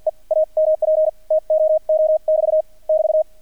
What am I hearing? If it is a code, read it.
Morse code, heard as a keyed tone, about 48 wpm, spaced as for 41 wpm, 630 Hz. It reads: E AMP TOO<BT> <BT>
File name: archived / CRSnet